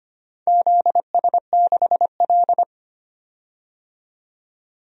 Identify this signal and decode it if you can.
Morse code, heard as a keyed tone, about 25 wpm, 700 Hz. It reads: ZS6L
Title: DXCC-100 Call Signs - 4 Characters - 25wpm